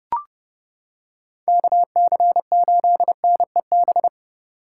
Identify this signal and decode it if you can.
Morse code, heard as a keyed tone, about 30 wpm, 700 Hz. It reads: KC8NEB